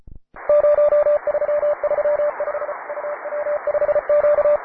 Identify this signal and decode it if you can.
Morse code, heard as a keyed tone, about 35 wpm, 590 Hz, with radio noise. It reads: 0335 UW5Q
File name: wae / cw advanced / py2na cw